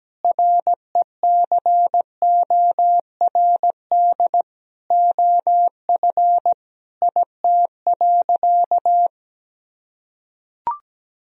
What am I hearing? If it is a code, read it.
Morse code, heard as a keyed tone, about 17 wpm, 700 Hz. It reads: RECORD OF IT.
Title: Sentences from Top 1000 Words - Review Part 3 - 17wpm